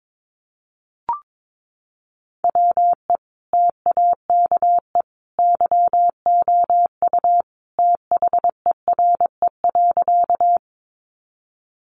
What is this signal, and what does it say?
Morse code, heard as a keyed tone, about 22 wpm, 700 Hz. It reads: WE TAKE YOU THERE.
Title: A Sound Of Thunder 22wpm